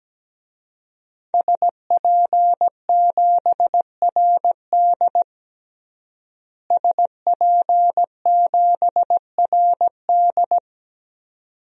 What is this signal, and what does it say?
Morse code, heard as a keyed tone, about 17 wpm, 700 Hz. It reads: SP7RD SP7RD
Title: DXCC-100 Call Signs - 5 Characters - Repeated 2x - 17wpm